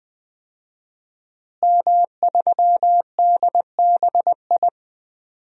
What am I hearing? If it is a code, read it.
Morse code, heard as a keyed tone, about 20 wpm, 700 Hz. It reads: M3DBI
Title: DXCC-100 Call Signs - 5 Characters - Repeated 2x - 20wpm